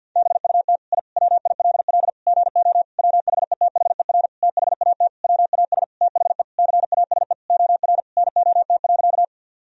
Morse code, heard as a keyed tone, about 50 wpm, 700 Hz. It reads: BUT I WILL GO WHETHER THAT WAS THE CASE OR NOT.
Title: Sentences from Top 1000 Words - Review Part 3 - 50wpm